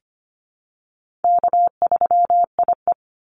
Morse code, heard as a keyed tone, about 25 wpm, 700 Hz. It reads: K3IE
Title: Call Signs - 25wpm